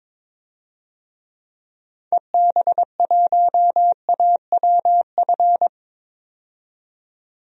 Morse code, heard as a keyed tone, about 22 wpm, 700 Hz. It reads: EB1AWF
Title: DXCC-100 Call Signs - 6 Characters - 22wpm